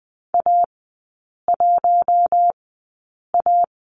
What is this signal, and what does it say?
Morse code, heard as a keyed tone, about 20 wpm, spaced as for 8 wpm, 700 Hz. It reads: A1A
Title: 041 Lesson 8e - Number 1 - Three Characters 20wpm